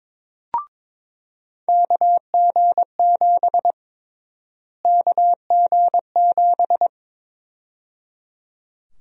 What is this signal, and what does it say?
Morse code, heard as a keyed tone, about 22 wpm, 700 Hz. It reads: KG7 KG7
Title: USA Call Sign Prefixes - Repeated 2x - 22wpm